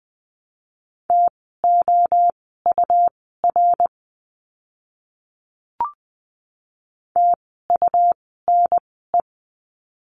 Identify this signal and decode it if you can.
Morse code, heard as a keyed tone, about 20 wpm, spaced as for 14 wpm, 700 Hz. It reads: TOUR TUNE
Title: 101 Lesson 17f - Letter u - Words 20wpm